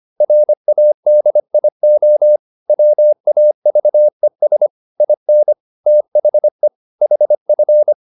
Morse code, heard as a keyed tone, about 25 wpm, 600 Hz. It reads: RADIO WAVES IN THE HF
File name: SS211 25